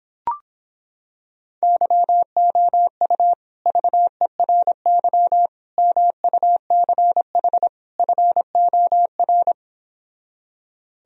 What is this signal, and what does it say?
Morse code, heard as a keyed tone, about 26 wpm, 700 Hz. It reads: YOU VERY MUCH FOR